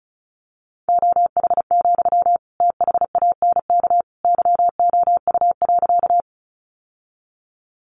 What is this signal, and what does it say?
Morse code, heard as a keyed tone, about 34 wpm, 700 Hz. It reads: OH, THANK YOU.
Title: Sentences from Top 600 Words 35wpm